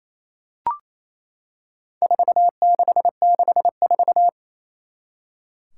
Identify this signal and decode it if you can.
Morse code, heard as a keyed tone, about 28 wpm, 700 Hz. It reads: 4664
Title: Four Challenging Letter-Number Combinations 28wpm